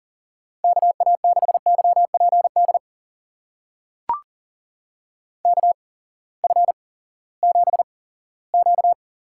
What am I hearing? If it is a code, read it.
Morse code, heard as a keyed tone, about 40 wpm, spaced as for 20 wpm, 700 Hz. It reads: KA6YPD KF7Q
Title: USA Call Signs - 6 Characters - 8x Character Spacing - 40wpm